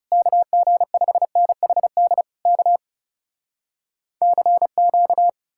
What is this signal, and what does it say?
Morse code, heard as a keyed tone, about 40 wpm, 700 Hz. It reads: KG5NHD K CQ